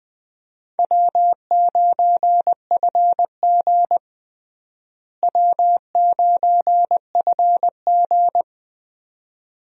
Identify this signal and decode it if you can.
Morse code, heard as a keyed tone, about 20 wpm, 700 Hz. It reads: W9FG W9FG
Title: USA Call Signs - 4 Characters - Repeated 2x - 20wpm